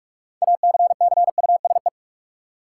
Morse code, heard as a keyed tone, about 43 wpm, 700 Hz. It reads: ACCUSE
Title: Six Letter Words - 45wpm